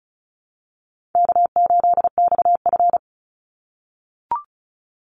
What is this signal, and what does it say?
Morse code, heard as a keyed tone, about 34 wpm, 700 Hz. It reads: K8XF
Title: Call Signs - 35wpm